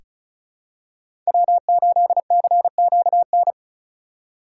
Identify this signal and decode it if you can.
Morse code, heard as a keyed tone, about 36 wpm, 700 Hz. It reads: W8CQN
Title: USA Call Signs - 2x Character Spacing - 35wpm